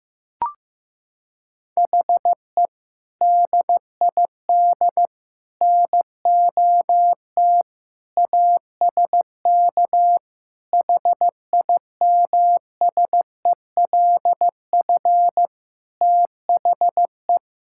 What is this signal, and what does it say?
Morse code, heard as a keyed tone, about 16 wpm, 700 Hz. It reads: HE DID NOT ASK HIMSELF THE
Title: Sentences from Top 1600 Words 15wpm